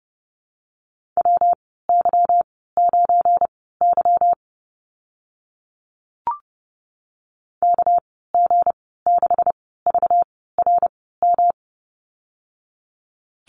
Morse code, heard as a keyed tone, about 30 wpm, spaced as for 17 wpm, 700 Hz. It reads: WY9Y KG6VRM
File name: Lesson 37g - Letter x - Call Signs 30wpm